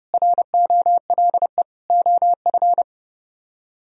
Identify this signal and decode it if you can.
Morse code, heard as a keyed tone, about 30 wpm, 700 Hz. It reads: ROLE OF